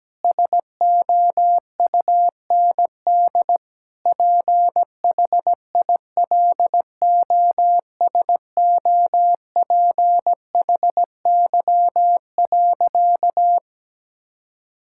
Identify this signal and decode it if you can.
Morse code, heard as a keyed tone, about 17 wpm, 700 Hz. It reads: SOUND PHILOSOPHY.
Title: Sentences from Top 1000 Words - Review Part 6 - 17wpm